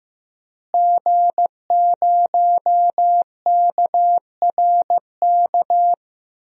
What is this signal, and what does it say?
Morse code, heard as a keyed tone, about 16 wpm, 700 Hz. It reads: G0KRK
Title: DXCC-100 Call Signs - 5 Characters - Repeated 3x - 15wpm